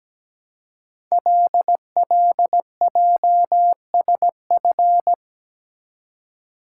Morse code, heard as a keyed tone, about 17 wpm, 700 Hz. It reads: LLJSF
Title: Code Groups - 5 Letters - 17wpm